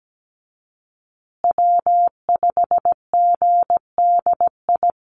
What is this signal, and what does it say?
Morse code, heard as a keyed tone, about 17 wpm, 700 Hz. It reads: W5GDI
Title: USA Call Signs Repeated 2x 17wpm